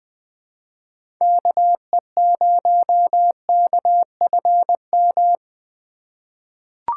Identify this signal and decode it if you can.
Morse code, heard as a keyed tone, about 20 wpm, 700 Hz. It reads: KE0KFM
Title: USA Call Signs - 6 Characters - 3x Character Spacing - 20wpm